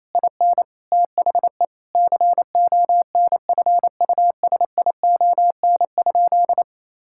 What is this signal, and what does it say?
Morse code, heard as a keyed tone, about 28 wpm, 700 Hz. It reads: IN THE CONFUSION?
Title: Sentences from Top 1600 Words 28wpm